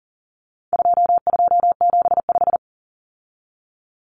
Morse code, heard as a keyed tone, about 40 wpm, 700 Hz. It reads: 2275
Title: Common Cut Numbers - 4-Digit 40wpm